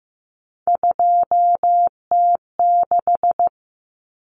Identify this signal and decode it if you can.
Morse code, heard as a keyed tone, about 15 wpm, 700 Hz. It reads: 2T6
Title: Common Cut Numbers - 3-Digit 15wpm